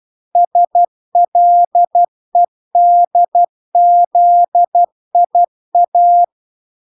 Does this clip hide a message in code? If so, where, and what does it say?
Yes: Morse code from 0.4 to 6.2 s, heard as a keyed tone, about 12 wpm, 700 Hz. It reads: SLEDZIA